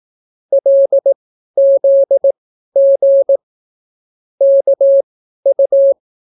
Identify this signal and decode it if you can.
Morse code, heard as a keyed tone, about 18 wpm, spaced as for 12 wpm, 550 Hz. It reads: LZG KU